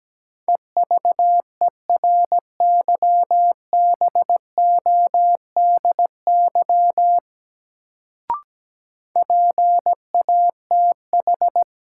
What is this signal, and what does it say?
Morse code, heard as a keyed tone, about 17 wpm, 700 Hz. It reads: EVERYBODY PATH
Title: Sentences from Top 1400 Words - New Word Drill Alt - 17wpm